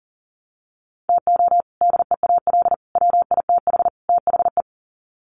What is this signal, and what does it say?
Morse code, heard as a keyed tone, about 37 wpm, 700 Hz. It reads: TO DEAL WITH THE
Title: Sets of 4 Words 40wpm